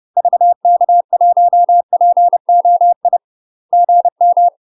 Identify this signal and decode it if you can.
Morse code, heard as a keyed tone, about 30 wpm, 700 Hz. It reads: VK1POI GM
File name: UK MASTER QSOs TRAINING / 30WPM / UK MASTER QSOs 0029